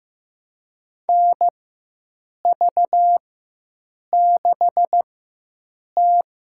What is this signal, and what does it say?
Morse code, heard as a keyed tone, about 16 wpm, spaced as for 7 wpm, 700 Hz. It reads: NV6T
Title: USA Call Signs - 4x Character Spacing - 15wpm